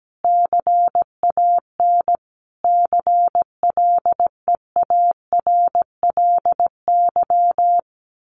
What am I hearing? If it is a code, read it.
Morse code, heard as a keyed tone, about 17 wpm, 700 Hz. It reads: CAN CLEARLY
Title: Sentences from Top 1400 Words 17wpm